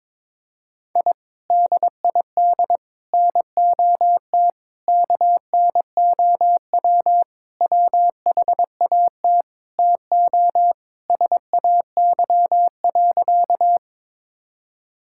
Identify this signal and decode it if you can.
Morse code, heard as a keyed tone, about 22 wpm, 700 Hz. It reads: I DID NOT KNOW WHAT TO SAY.
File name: practice / 22wpm / Sentences with 4-Word Phrases - 22wpm